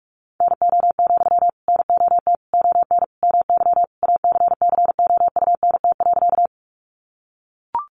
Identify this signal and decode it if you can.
Morse code, heard as a keyed tone, about 44 wpm, 700 Hz. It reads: NO, NOT ON MY ACCOUNT.
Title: Sentences from Top 900 Words 45wpm